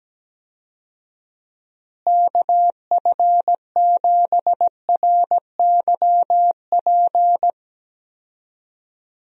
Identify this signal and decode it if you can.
Morse code, heard as a keyed tone, about 17 wpm, 700 Hz. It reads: KF7RYP
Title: USA Call Signs - 6 Characters - Repeated 3x - 17wpm